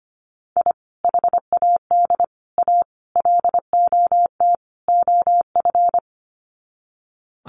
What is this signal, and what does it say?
Morse code, heard as a keyed tone, about 25 wpm, 700 Hz. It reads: I HAD A LOT OF
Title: Sets of 5 Words Encore 25wpm